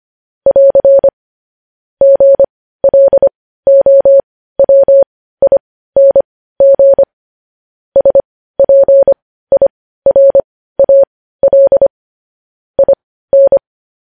Morse code, heard as a keyed tone, about 25 wpm, spaced as for 15 wpm, 550 Hz. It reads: <AR> GLOWING SPIRAL IN
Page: News Headlines in Morse Code at 15 WPM